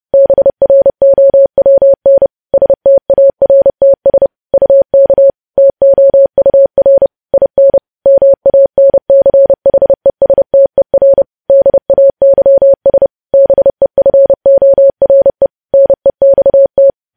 Morse code, heard as a keyed tone, about 30 wpm, 550 Hz. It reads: BROWN STARTS UK TOUR IN MANCHESTER DAYS BEFORE NEXT
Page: news headlines in morse code at 30 wpm